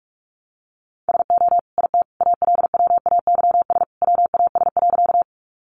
Morse code, heard as a keyed tone, about 44 wpm, 700 Hz. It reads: SO IT ALWAYS WAS.
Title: Sentences from Top 300 Words 45wpm